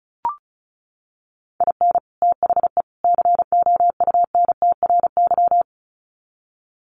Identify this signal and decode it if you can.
Morse code, heard as a keyed tone, about 34 wpm, 700 Hz. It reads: IN THE COUNTRY
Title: Sets of 3 Words 35wpm